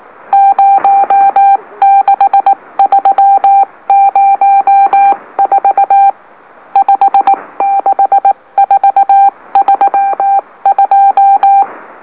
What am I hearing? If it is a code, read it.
Morse code, heard as a keyed tone, about 18 wpm, 790 Hz. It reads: 06304 56432
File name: M04- -2345utc-LOLO